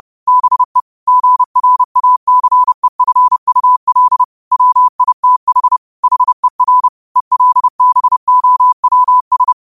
Morse code, heard as a keyed tone, about 30 wpm, 1000 Hz. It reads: BE GRACEFUL WITH HER ELBOWS
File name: qotdfiles / 30wpm